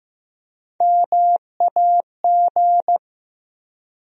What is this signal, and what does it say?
Morse code, heard as a keyed tone, about 16 wpm, 700 Hz. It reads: MAG
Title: USA Call Sign Postfixes - 3 Letters 15wpm